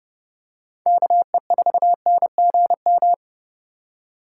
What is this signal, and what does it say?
Morse code, heard as a keyed tone, about 30 wpm, 700 Hz. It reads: KE4NGM
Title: USA Call Signs - 6 Characters - 2x Character Spacing - 30wpm